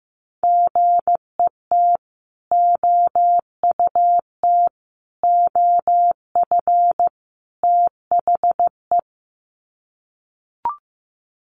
Morse code, heard as a keyed tone, about 15 wpm, 700 Hz. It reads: GET OUT OF THE
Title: Sets of 4 Words 15wpm